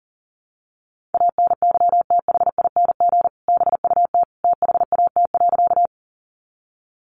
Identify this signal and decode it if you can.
Morse code, heard as a keyed tone, about 40 wpm, 700 Hz. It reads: ANYTHING BUT THAT.
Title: Sentences from Top 1000 Words 40wpm